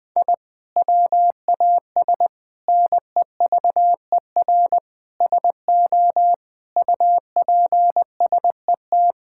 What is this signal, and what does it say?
Morse code, heard as a keyed tone, about 21 wpm, 700 Hz. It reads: I WAS NEVER SO UPSET
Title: Sentences from Top 1000 Words - Review Part 5 - 20wpm